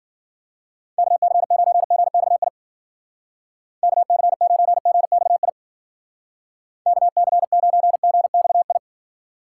Morse code, heard as a keyed tone, about 52 wpm, 700 Hz. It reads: KC9GXI KC9GXI KC9GXI